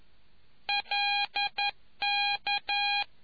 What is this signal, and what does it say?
Morse code, heard as a keyed tone, about 11 wpm, 810 Hz. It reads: LK